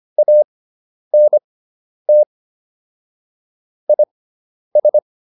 Morse code, heard as a keyed tone, about 25 wpm, spaced as for 10 wpm, 600 Hz. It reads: ANT IS